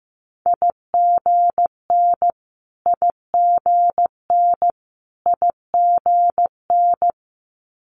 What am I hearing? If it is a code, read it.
Morse code, heard as a keyed tone, about 15 wpm, 700 Hz. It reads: IGN IGN IGN
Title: Multi-letter Phonemes Repeated 3x 15wpm